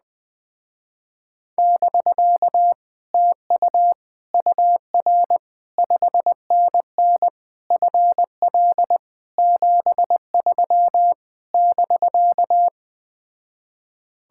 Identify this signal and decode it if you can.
Morse code, heard as a keyed tone, about 20 wpm, 700 Hz. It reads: <BK> TU UR 5NN FL 73 <BK>